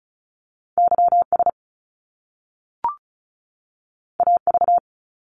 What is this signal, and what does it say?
Morse code, heard as a keyed tone, about 36 wpm, 700 Hz. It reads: YS AV
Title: Top Two Letter Combinations in Words 35wpm